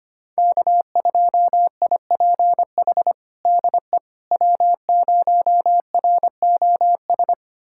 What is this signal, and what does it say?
Morse code, heard as a keyed tone, about 25 wpm, 700 Hz. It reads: K2IPH DE W0ROS
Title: USA Callsign DE Callsign - 5 Characters - 25wpm